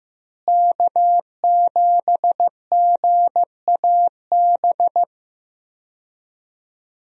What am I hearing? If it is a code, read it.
Morse code, heard as a keyed tone, about 15 wpm, 700 Hz. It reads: K7GAB
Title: USA Call Signs - 5 Characters - Repeated 2x - 15wpm